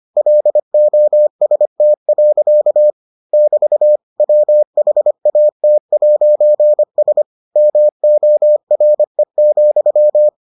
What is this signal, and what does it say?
Morse code, heard as a keyed tone, about 25 wpm, 600 Hz. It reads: LOST. <BT> WHAT'S MORE,